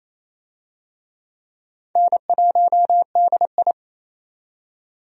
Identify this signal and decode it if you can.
Morse code, heard as a keyed tone, about 28 wpm, 700 Hz. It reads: N1DI
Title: USA Call Signs - 4 Characters - 28wpm